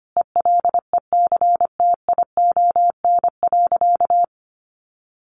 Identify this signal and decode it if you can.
Morse code, heard as a keyed tone, about 25 wpm, 700 Hz. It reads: ELECTION.
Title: Sentences from Top 1500 Words 25wpm